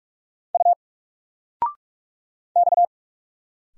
Morse code, heard as a keyed tone, about 46 wpm, 700 Hz. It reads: U X
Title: Single Letter-Number 45wpm